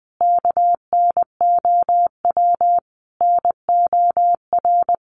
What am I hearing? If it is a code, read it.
Morse code, heard as a keyed tone, about 20 wpm, 700 Hz. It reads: KNOW NOR
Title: Sentences from Top 1500 Words 20wpm